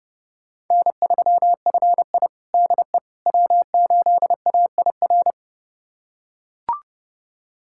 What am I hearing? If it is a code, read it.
Morse code, heard as a keyed tone, about 30 wpm, 700 Hz. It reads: N3FI DE W8AIR
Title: USA Callsign DE Callsign 30wpm